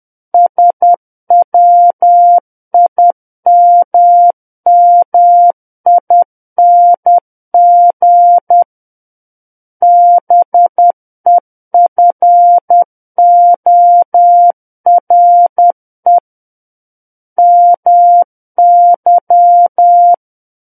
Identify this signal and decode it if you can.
Morse code, heard as a keyed tone, about 10 wpm, 700 Hz. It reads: SWIMMING BEFORE MY
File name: Book One / 01-Chapter 1 - 10 WPM